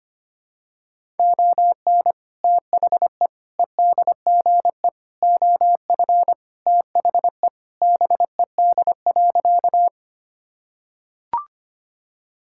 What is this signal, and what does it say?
Morse code, heard as a keyed tone, about 25 wpm, 700 Hz. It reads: ON THE EDGE OF THE BED.
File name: practice / 25wpm / Sentences with 3-Word Phrases - 25wpm